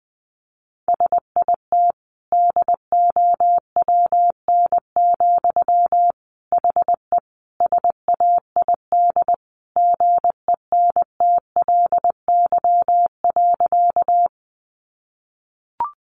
Morse code, heard as a keyed tone, about 20 wpm, 700 Hz. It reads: SIT DOWN, HE SAID GENTLY.
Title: Sentences from Top 1500 Words 20wpm